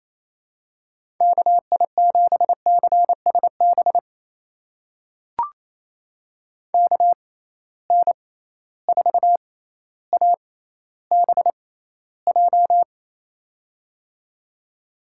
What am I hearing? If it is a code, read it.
Morse code, heard as a keyed tone, about 28 wpm, spaced as for 14 wpm, 700 Hz. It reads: KI7CSB KN4ABJ
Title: USA Call Signs - 6 Characters - 6x Character Spacing - 28wpm